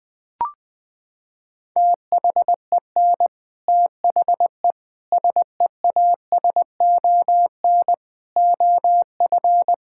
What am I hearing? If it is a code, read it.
Morse code, heard as a keyed tone, about 20 wpm, 700 Hz. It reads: THEN THE SEASON OF
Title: Sentences from Top 1600 Words 20wpm